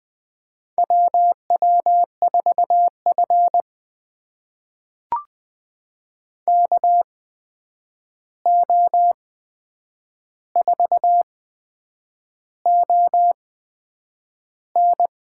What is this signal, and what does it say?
Morse code, heard as a keyed tone, about 21 wpm, spaced as for 8 wpm, 700 Hz. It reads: WW4F KO4ON
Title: USA Call Signs - 8x Character Spacing - 20wpm